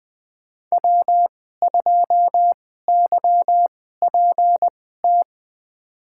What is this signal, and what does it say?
Morse code, heard as a keyed tone, about 20 wpm, spaced as for 14 wpm, 700 Hz. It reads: W2YPT
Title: USA Call Signs - 2x Character Spacing - 20wpm